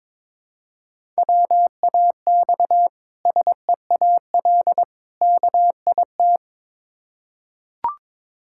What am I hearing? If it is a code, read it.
Morse code, heard as a keyed tone, about 22 wpm, 690 Hz. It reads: WAX SEAL KIT
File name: Difficult Words with Hints - 22wpm